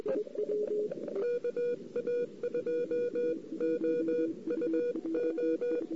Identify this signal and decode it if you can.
Morse code, heard as a keyed tone, about 21 wpm, 480 Hz. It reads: KA2OUO
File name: CW / 2008-ss-cw-k5tr-op-n5rz -05:00